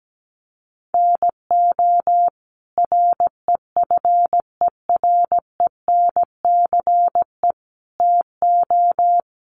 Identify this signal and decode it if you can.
Morse code, heard as a keyed tone, about 17 wpm, 700 Hz. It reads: NO REFERENCE TO
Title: Sentences from Top 1200 Words 17wpm